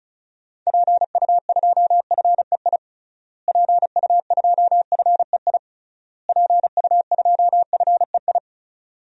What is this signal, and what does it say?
Morse code, heard as a keyed tone, about 34 wpm, 700 Hz. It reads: PU2FEI PU2FEI PU2FEI
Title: DXCC-100 Call Signs - 6 Characters - Repeated 3x - 35wpm